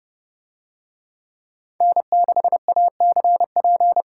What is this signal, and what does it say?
Morse code, heard as a keyed tone, about 28 wpm, 700 Hz. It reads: N6ACP